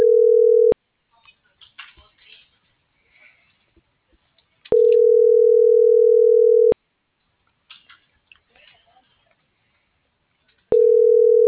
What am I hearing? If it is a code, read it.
Ringback tone (North America), 440 + 480 Hz.